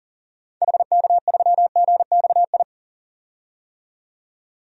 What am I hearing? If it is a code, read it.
Morse code, heard as a keyed tone, about 40 wpm, 700 Hz. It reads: HK3GXI
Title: DXCC-100 Call Signs - 6 Characters - 40wpm